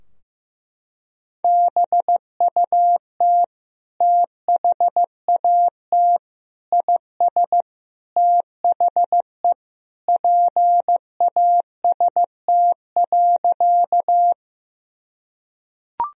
Morse code, heard as a keyed tone, about 15 wpm, 700 Hz. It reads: BUT THAT IS THE PAST.
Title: Sentences from Top 1000 Words - Review Part 3 - 15wpm